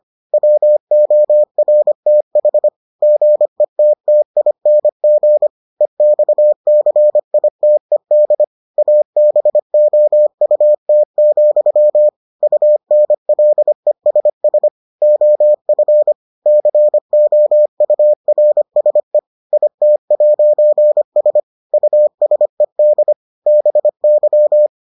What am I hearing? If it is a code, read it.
Morse code, heard as a keyed tone, about 25 wpm, 600 Hz. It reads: WORTH GETTING EXCITED ABOUT, UNLESS OF COURSE IT'S USED BY